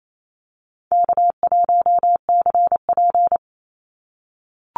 Morse code, heard as a keyed tone, about 28 wpm, 700 Hz. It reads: K1CP
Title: Call Signs - 28wpm